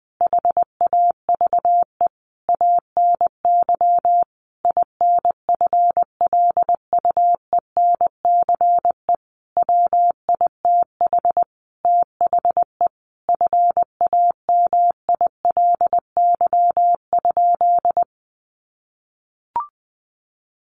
Morse code, heard as a keyed tone, about 20 wpm, 700 Hz. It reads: HAVE ANY INFLUENCE WITH THE FAMILY?
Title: Sentences from Top 1100 Words 20wpm